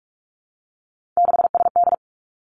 Morse code, heard as a keyed tone, about 46 wpm, 700 Hz. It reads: 6SD
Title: Three Challenging Letter-Number Combinations 45wpm